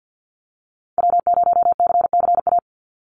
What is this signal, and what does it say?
Morse code, heard as a keyed tone, about 50 wpm, 700 Hz. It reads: W0CCA
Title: NA SOTA Activator Callsigns 50wpm